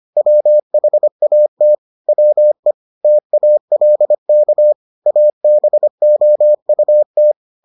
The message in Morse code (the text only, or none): WHAT WE TALK ABOUT